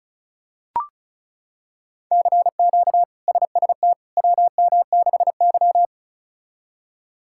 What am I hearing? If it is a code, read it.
Morse code, heard as a keyed tone, about 36 wpm, 700 Hz. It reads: CQ SST WM6Y
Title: K1USN SST QSO Practice - 35wpm